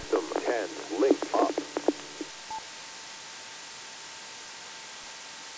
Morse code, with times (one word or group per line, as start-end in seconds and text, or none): none